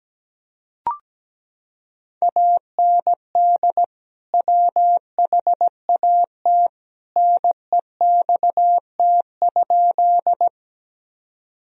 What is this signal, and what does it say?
Morse code, heard as a keyed tone, about 17 wpm, 700 Hz. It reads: AND WHAT NEXT?
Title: Sentences from Top 1000 Words - Review Part 1 - 17wpm